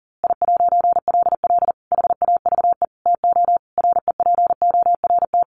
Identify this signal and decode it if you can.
Morse code, heard as a keyed tone, about 37 wpm, 700 Hz. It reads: I'LL HAVE TO REPORT
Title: A Sound Of Thunder 40wpm